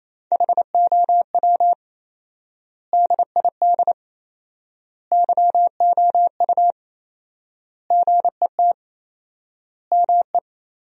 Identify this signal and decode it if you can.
Morse code, heard as a keyed tone, about 28 wpm, 700 Hz. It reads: HOW DID YOU GET ME
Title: 4x Word-Spacing - Sentences from Top 200 Words - 28wpm